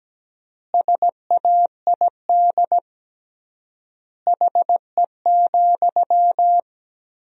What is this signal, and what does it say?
Morse code, heard as a keyed tone, about 17 wpm, 700 Hz. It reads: SAID HE,
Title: 3x Word-Spacing - Sentences from Top 500 Words - 17wpm